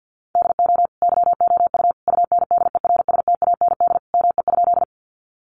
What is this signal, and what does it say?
Morse code, heard as a keyed tone, about 50 wpm, 700 Hz. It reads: DO YOU UNDERSTAND ME?
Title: Sentences from Top 600 Words 50wpm